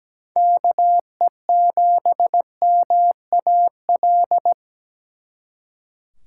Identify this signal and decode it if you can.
Morse code, heard as a keyed tone, about 20 wpm, 700 Hz. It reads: KE7MAL
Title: Speed Racing - USA 2x3 Call Signs 17wpm